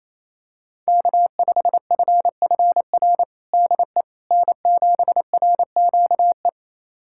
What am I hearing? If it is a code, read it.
Morse code, heard as a keyed tone, about 28 wpm, 700 Hz. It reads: K5FFR DE N7RQE
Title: USA Callsign DE Callsign - 5 Characters - 28wpm